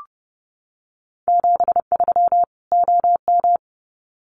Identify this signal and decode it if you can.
Morse code, heard as a keyed tone, about 30 wpm, 700 Hz. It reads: 73 OM
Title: Common QSO Phrases 30wpm